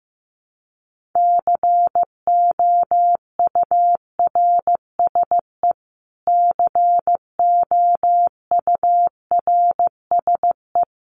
Morse code, heard as a keyed tone, about 15 wpm, 700 Hz. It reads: COURSE COURSE
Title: Top 500 Words Repeated 15wpm